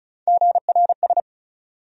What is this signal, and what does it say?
Morse code, heard as a keyed tone, about 36 wpm, 700 Hz. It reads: GRS
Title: USA Call Sign Postfixes - 3 Letters 35wpm